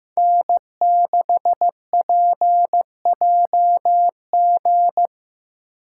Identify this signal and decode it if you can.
Morse code, heard as a keyed tone, about 15 wpm, 700 Hz. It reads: N6PJG
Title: USA Call Signs - 5 Characters - 3x Character Spacing - 15wpm